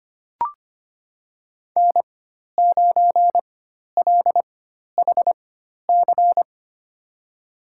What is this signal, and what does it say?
Morse code, heard as a keyed tone, about 25 wpm, spaced as for 12 wpm, 700 Hz. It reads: N9LHC
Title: USA Call Signs - 5 Characters - 4x Character Spacing - 25wpm